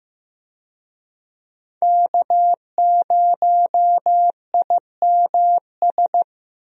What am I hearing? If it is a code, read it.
Morse code, heard as a keyed tone, about 15 wpm, 700 Hz. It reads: K0IMS
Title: USA Call Signs - 5 Characters - Repeated 2x - 15wpm